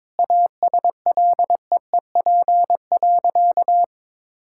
Morse code, heard as a keyed tone, about 22 wpm, 700 Hz. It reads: ASLEEP.
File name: practice / 22wpm / Sentences with 3-Word Phrases - 22wpm